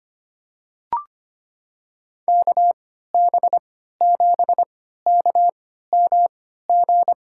Morse code, heard as a keyed tone, about 25 wpm, spaced as for 14 wpm, 700 Hz. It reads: KB7KMG
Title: USA Call Signs - 6 Characters - 3x Character Spacing - 25wpm